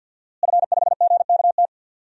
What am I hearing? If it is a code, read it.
Morse code, heard as a keyed tone, about 46 wpm, 700 Hz. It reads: F5GKT
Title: DXCC-100 Call Signs - 5 Characters - Repeated 3x - 50wpm